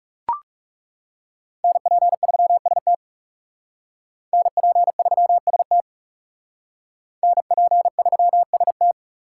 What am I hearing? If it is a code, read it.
Morse code, heard as a keyed tone, about 37 wpm, 700 Hz. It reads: NP3ST NP3ST NP3ST
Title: Speed Racing - USA 2x2 Call Signs 30wpm